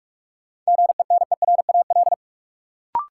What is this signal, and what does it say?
Morse code, heard as a keyed tone, about 44 wpm, 700 Hz. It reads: GENERAL
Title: Sentences from Top 600 Words - New Word Drill - 45wpm